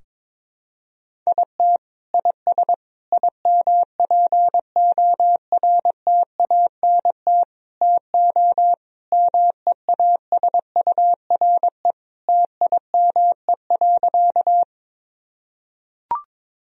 Morse code, heard as a keyed tone, about 22 wpm, 700 Hz. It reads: IT IS IMPORTANT TO MEASURE TIME.
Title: Sentences from Top 1000 Words - Review Part 2 - 22wpm